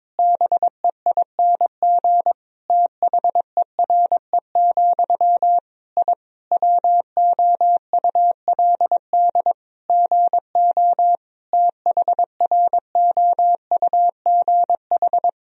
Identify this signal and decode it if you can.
Morse code, heard as a keyed tone, about 21 wpm, 700 Hz. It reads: BEING THERE, I WOULD GO THROUGH
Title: Sentences from Top 1000 Words - Review Part 3 - 22wpm